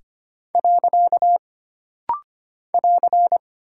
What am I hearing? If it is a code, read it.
Morse code, heard as a keyed tone, about 25 wpm, 700 Hz. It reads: . <AR>
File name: Single Punctuation-Prosign - Rapid-Fire - 25wpm